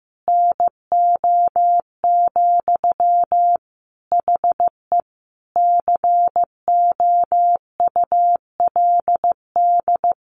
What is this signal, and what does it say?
Morse code, heard as a keyed tone, about 15 wpm, 700 Hz. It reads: NO, HE COULD
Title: Sentences from Top 1400 Words 15wpm